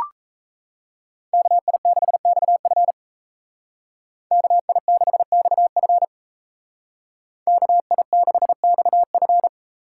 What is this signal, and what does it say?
Morse code, heard as a keyed tone, about 37 wpm, 700 Hz. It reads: KI6XF KI6XF KI6XF